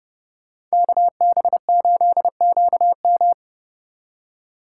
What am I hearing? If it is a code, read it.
Morse code, heard as a keyed tone, about 36 wpm, 700 Hz. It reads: KB8QM
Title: Speed Racing - USA 2x2 Call Signs 30wpm